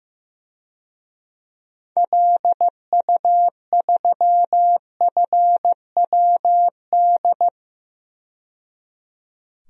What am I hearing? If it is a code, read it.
Morse code, heard as a keyed tone, about 16 wpm, 700 Hz. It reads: LU3FWD
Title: DXCC-100 Call Signs - 6 Characters - 15wpm